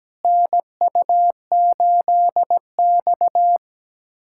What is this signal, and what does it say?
Morse code, heard as a keyed tone, about 17 wpm, 700 Hz. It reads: NU8X
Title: USA Callsign DE Callsign - 4 Characters - 17wpm